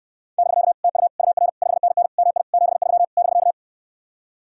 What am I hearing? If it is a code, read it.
Morse code, heard as a keyed tone, about 67 wpm, 700 Hz. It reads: <BK> TU UR 5NN MI 73 <BK>